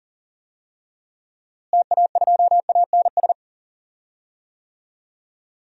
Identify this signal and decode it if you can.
Morse code, heard as a keyed tone, about 40 wpm, 700 Hz. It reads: TA2ANS
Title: DXCC-100 Call Signs - 6 Characters - Repeated 3x - 40wpm